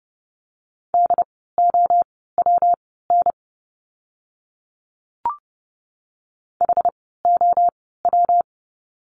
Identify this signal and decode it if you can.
Morse code, heard as a keyed tone, about 30 wpm, spaced as for 17 wpm, 700 Hz. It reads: DOWN HOW
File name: course / 30:10wpm / 115 Lesson 19f - Letter w - Words 30wpm